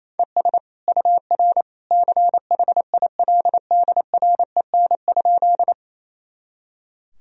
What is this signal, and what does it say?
Morse code, heard as a keyed tone, about 28 wpm, 700 Hz. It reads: ES UR CHILDREN?
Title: Sentences from Top 300 Words and Abbreviations 28wpm